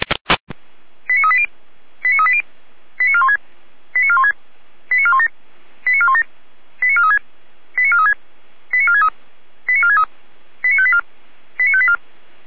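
Five-tone selective call (ZVEI), heard as quick runs of five tones, tones of about 70 ms. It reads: 89380 89380 89416 89416 89417 89417 89436 89436 89463 89463 89584 89584
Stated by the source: ZVEI five-tone selective call 89401, received via SDR.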